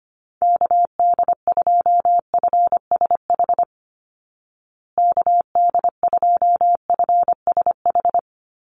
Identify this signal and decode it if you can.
Morse code, heard as a keyed tone, about 25 wpm, 700 Hz. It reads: KD2FSH KD2FSH
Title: Call Signs Repeated 2x - 25wpm